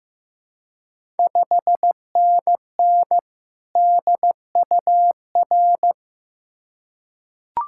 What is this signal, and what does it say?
Morse code, heard as a keyed tone, about 15 wpm, 700 Hz. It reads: 5NN DUR
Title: Ontario QSO Party Exchange 15wpm